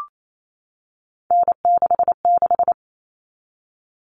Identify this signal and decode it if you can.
Morse code, heard as a keyed tone, about 28 wpm, 700 Hz. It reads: N66
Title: All Cut Numbers - 3-Digit 28wpm